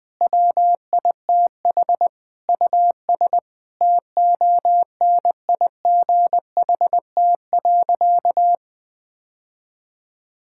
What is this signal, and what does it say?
Morse code, heard as a keyed tone, about 21 wpm, 700 Hz. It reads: WITH US TONIGHT.
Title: Sentences with 5-Word Phrases - 20wpm